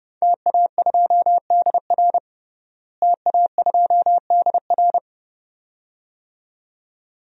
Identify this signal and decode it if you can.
Morse code, heard as a keyed tone, about 31 wpm, 700 Hz. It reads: TA2DR TA2DR
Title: DXCC-100 Call Signs - 5 Characters - Repeated 2x - 30wpm